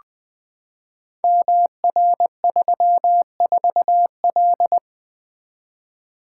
Morse code, heard as a keyed tone, about 21 wpm, 700 Hz. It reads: MR34L
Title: Code Groups - 5 Letter:Numbers - 20wpm